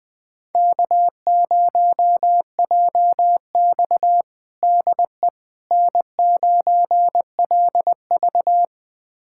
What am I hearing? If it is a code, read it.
Morse code, heard as a keyed tone, about 20 wpm, 700 Hz. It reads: K0JX DE N9LV
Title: USA Callsign DE Callsign - 4 Characters - 20wpm